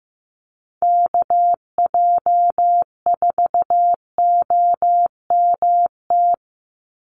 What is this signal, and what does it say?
Morse code, heard as a keyed tone, about 15 wpm, 700 Hz. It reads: KJ4OMT
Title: USA Call Signs 15wpm